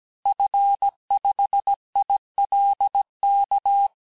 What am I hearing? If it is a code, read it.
Morse code, heard as a keyed tone, about 17 wpm, 800 Hz. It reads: F5ILK
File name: corrige-cours-1581-fin-du-QSO-vitesse-16-mpm